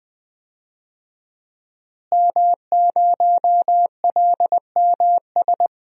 Morse code, heard as a keyed tone, about 20 wpm, 700 Hz. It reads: M0LMS
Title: DXCC-100 Call Signs - 5 Characters - Repeated 2x - 20wpm